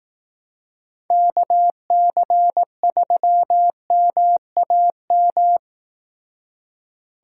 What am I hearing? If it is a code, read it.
Morse code, heard as a keyed tone, about 17 wpm, 700 Hz. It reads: KC3MAM
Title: Speed Racing - USA 2x3 Call Signs 15wpm